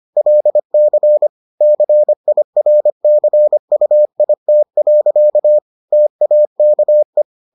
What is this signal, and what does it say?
Morse code, heard as a keyed tone, about 25 wpm, 600 Hz. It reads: LC CIRCUIT. TAKE